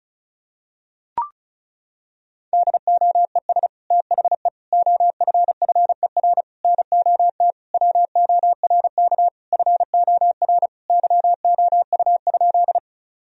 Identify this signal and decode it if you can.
Morse code, heard as a keyed tone, about 34 wpm, 700 Hz. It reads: DOES THE OFFER NOT WORK FOR YOU?
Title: Sentences from Top 1000 Words - Review Part 4 - 35wpm